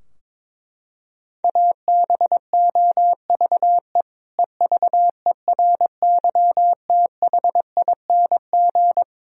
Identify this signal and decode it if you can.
Morse code, heard as a keyed tone, about 22 wpm, 700 Hz. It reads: ABOVE EVERYTHING
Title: Sentences from Top 1000 Words - Review Part 2 - 22wpm